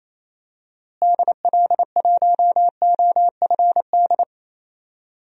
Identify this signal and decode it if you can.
Morse code, heard as a keyed tone, about 28 wpm, 700 Hz. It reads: DL1OFD
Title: DXCC-100 Call Signs - 6 Characters - Repeated 3x - 28wpm